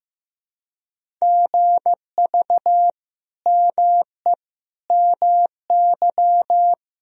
Morse code, heard as a keyed tone, about 15 wpm, 700 Hz. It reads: GV ME MY
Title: Sentences from Top 200 Words and Abbreviations 15wpm